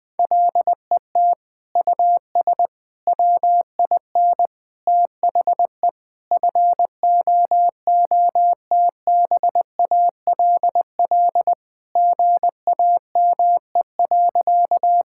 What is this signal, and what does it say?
Morse code, heard as a keyed tone, about 21 wpm, 700 Hz. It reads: LET US WIN THE FOOTBALL GAME.
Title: Sentences from Top 1600 Words 20wpm